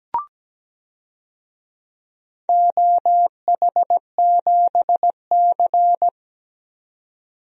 Morse code, heard as a keyed tone, about 17 wpm, 700 Hz. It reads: OH7C
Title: DXCC-100 Call Signs - 4 Characters - 17wpm